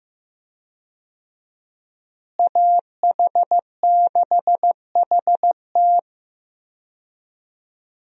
Morse code, heard as a keyed tone, about 15 wpm, 700 Hz. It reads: AH6HT
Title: USA Call Signs - 5 Characters - Repeated 3x - 15wpm